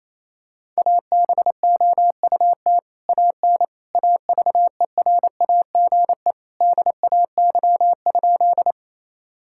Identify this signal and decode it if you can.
Morse code, heard as a keyed tone, about 28 wpm, 700 Hz. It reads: ABOUT AN AVERAGE DAY?
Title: Sentences from Top 1000 Words - Review Part 6 - 28wpm